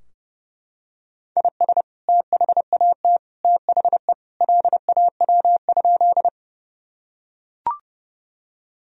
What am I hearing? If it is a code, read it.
Morse code, heard as a keyed tone, about 30 wpm, 700 Hz. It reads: IS THAT THE LAW?
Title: Sentences from Top 1000 Words - Review Part 3 - 30wpm